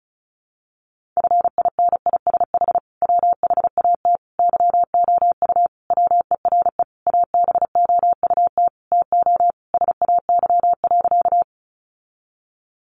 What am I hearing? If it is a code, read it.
Morse code, heard as a keyed tone, about 34 wpm, 700 Hz. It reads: FINISH WHAT YOU WERE ABOUT TO SAY.
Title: Sentences from Top 1200 Words 35wpm